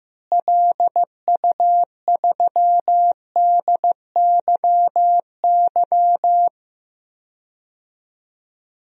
Morse code, heard as a keyed tone, about 15 wpm, 700 Hz. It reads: LU3DYY
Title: DXCC-100 Call Signs - 6 Characters - Repeated 2x - 15wpm